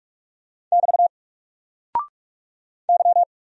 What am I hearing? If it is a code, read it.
Morse code, heard as a keyed tone, about 44 wpm, 700 Hz. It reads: <BT> Y
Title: Single Character (Letter:Number:Punctuation:Prosign) 45wpm